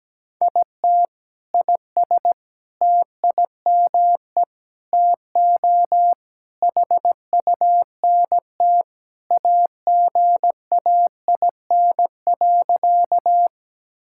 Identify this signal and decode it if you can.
Morse code, heard as a keyed tone, about 17 wpm, 700 Hz. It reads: IT IS TIME TO HUNT AGAIN.
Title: Sentences from Top 1600 Words 17wpm